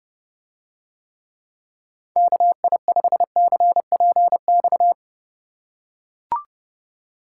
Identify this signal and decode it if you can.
Morse code, heard as a keyed tone, about 31 wpm, 700 Hz. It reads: KI5CPX